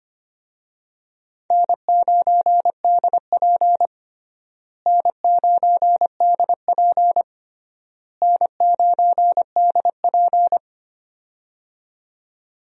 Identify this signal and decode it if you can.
Morse code, heard as a keyed tone, about 25 wpm, 700 Hz. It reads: N9DP N9DP N9DP
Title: USA Call Signs - 4 Characters - Repeated 3x - 25wpm